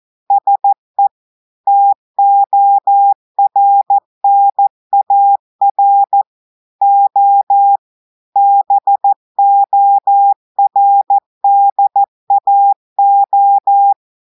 Morse code, heard as a keyed tone, about 14 wpm, 800 Hz. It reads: SE TORNAR O BORDAO
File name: audios / portugais / portugais texte en clair 2 vitesse 14 mots minutes